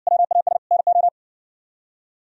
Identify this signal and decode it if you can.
Morse code, heard as a keyed tone, about 60 wpm, 700 Hz. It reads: WAS NO